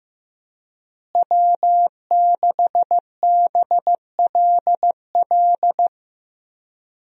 Morse code, heard as a keyed tone, about 15 wpm, 700 Hz. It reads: W6BLL
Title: USA Call Signs - 5 Characters - Repeated 2x - 15wpm